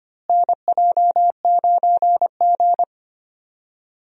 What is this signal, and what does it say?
Morse code, heard as a keyed tone, about 25 wpm, 700 Hz. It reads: NJ9G
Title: USA Call Signs - 4 Characters - Repeated 2x - 25wpm